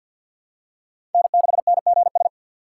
Morse code, heard as a keyed tone, about 47 wpm, 700 Hz. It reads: N6NGS
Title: USA Call Signs - 5 Characters - 2x Character Spacing - 50wpm